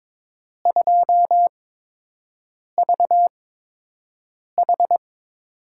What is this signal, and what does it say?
Morse code, heard as a keyed tone, about 22 wpm, spaced as for 6 wpm, 700 Hz. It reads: 2VH
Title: USA Call Signs - 4 Characters - 8x Character Spacing - 22wpm